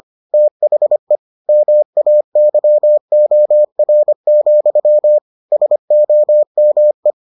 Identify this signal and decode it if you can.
Morse code, heard as a keyed tone, about 25 wpm, 600 Hz. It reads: THE MAYOR, SOME